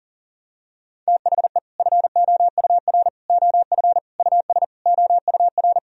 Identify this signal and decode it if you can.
Morse code, heard as a keyed tone, about 40 wpm, 700 Hz. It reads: THE FOUR OF US OUR
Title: Sentences from Top 1000 Words - Review Part 1 - 40wpm